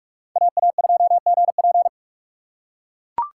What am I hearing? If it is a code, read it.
Morse code, heard as a keyed tone, about 42 wpm, 690 Hz. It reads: AA2GP
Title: USA Call Signs - 5 Characters - 2x Character Spacing - 45wpm